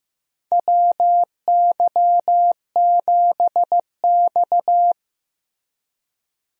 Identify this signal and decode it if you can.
Morse code, heard as a keyed tone, about 15 wpm, 700 Hz. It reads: WY7X
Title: USA Call Signs - 4 Characters - Repeated 3x - 15wpm